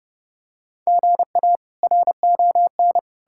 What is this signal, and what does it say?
Morse code, heard as a keyed tone, about 32 wpm, 700 Hz. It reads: GA RON
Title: K1USN SST QSO Practice - 30wpm